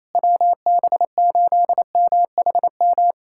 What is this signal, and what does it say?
Morse code, heard as a keyed tone, about 28 wpm, 700 Hz. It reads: WB8MHM
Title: USA Callsign DE Callsign - 6 Characters - 28wpm